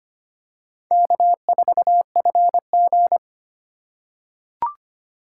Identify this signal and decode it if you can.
Morse code, heard as a keyed tone, about 25 wpm, 700 Hz. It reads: K4FG
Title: USA Call Signs - 4 Characters - 2x Character Spacing - 25wpm